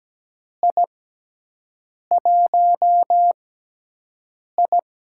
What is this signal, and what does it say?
Morse code, heard as a keyed tone, about 17 wpm, spaced as for 6 wpm, 700 Hz. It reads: I1I
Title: USA Call Signs - 4 Characters - 6x Character Spacing - 17wpm